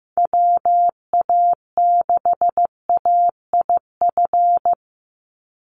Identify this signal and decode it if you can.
Morse code, heard as a keyed tone, about 15 wpm, 700 Hz. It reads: WA6AIF
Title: USA Call Signs 15wpm